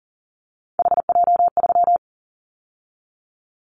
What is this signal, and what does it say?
Morse code, heard as a keyed tone, about 40 wpm, 700 Hz. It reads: HJ3
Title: Callsign Prefixes for Top 100 DXCC - 40wpm